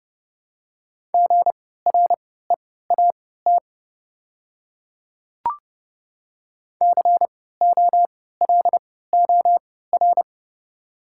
Morse code, heard as a keyed tone, about 30 wpm, spaced as for 17 wpm, 690 Hz. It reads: GREAT COLOR
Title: 3x Character Spacing - Top 500 Words - 30wpm